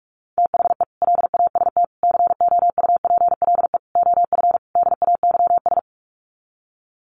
Morse code, heard as a keyed tone, about 44 wpm, 700 Hz. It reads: THE LAST COUPLE OF DAYS
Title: Sets of 5 Words Encore 45wpm